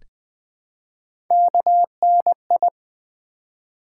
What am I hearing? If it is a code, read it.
Morse code, heard as a keyed tone, about 20 wpm, 700 Hz. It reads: KNI
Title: Build A Word - 1 Letter At A Time - 3 to 5-Letter Words - 20wpm